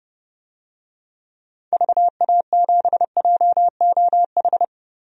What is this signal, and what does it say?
Morse code, heard as a keyed tone, about 30 wpm, 700 Hz. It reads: VA7JOH
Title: DXCC-100 Call Signs - 6 Characters - Repeated 2x - 30wpm